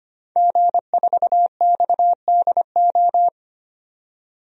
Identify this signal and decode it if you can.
Morse code, heard as a keyed tone, about 25 wpm, 700 Hz. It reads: G4XDO
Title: DXCC-100 Call Signs - 5 Characters - 25wpm